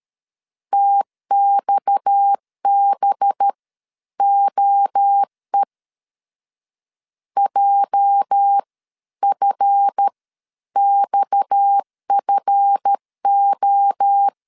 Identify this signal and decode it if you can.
Morse code, heard as a keyed tone, about 13 wpm, spaced as for 10 wpm, 800 Hz. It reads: TXBOE JFXFO
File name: LEZIONI / 8 YJ45 / 8 YJ 12wpm 2235